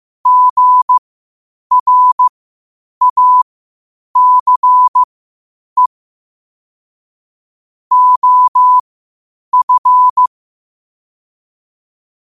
Morse code, heard as a keyed tone, about 15 wpm, spaced as for 8 wpm, 1000 Hz. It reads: GRACE OF
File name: qotdfiles / 8wpm